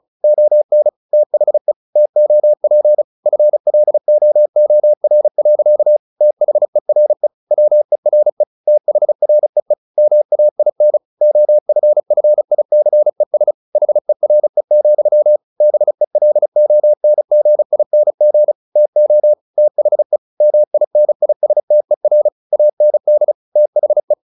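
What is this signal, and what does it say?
Morse code, heard as a keyed tone, about 34 wpm, 600 Hz. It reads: ON THE TOP FLOOR. THERE WERE THREE MAIN OFFICES HERE, BELONGING TO THE MINISTER AND THE
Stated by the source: Morse practice at 35 wpm from CW med Gnister, sending Ken Follett's novel Never in English.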